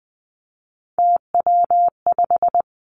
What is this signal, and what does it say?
Morse code, heard as a keyed tone, about 19 wpm, 700 Hz. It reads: TW5
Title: Three Challenging Letter-Number Combinations 20wpm